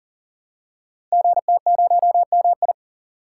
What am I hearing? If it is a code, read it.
Morse code, heard as a keyed tone, about 39 wpm, 690 Hz. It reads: GT0MI
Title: Code Groups - 5 Letter:Numbers - 40wpm